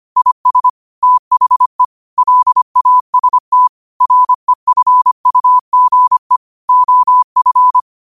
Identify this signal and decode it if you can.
Morse code, heard as a keyed tone, about 25 wpm, 1000 Hz. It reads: IS THE LAST REFUGE OF
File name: qotdfiles / 25wpm